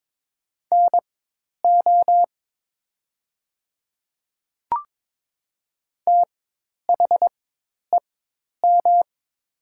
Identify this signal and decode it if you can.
Morse code, heard as a keyed tone, about 22 wpm, spaced as for 10 wpm, 700 Hz. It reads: NO THEM
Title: 4x Character Spacing - Top 100 Words - 22wpm